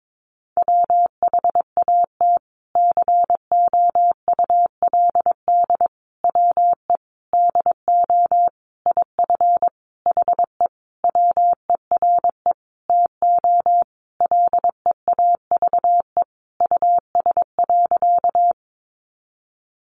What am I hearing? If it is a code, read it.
Morse code, heard as a keyed tone, about 22 wpm, 700 Hz. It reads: WHAT COULD WE DO IF HE WERE TO LEAVE US.
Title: Sentences from Top 400 Words 22wpm